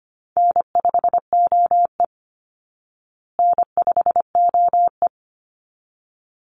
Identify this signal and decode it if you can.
Morse code, heard as a keyed tone, about 25 wpm, 700 Hz. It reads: N5OE N5OE
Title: Easy Call Signs Repeated 2x - 25wpm